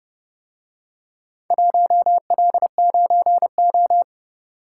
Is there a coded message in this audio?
Yes: Morse code from 1.5 to 4.0 s, heard as a keyed tone, about 31 wpm, 700 Hz. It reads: 1L9O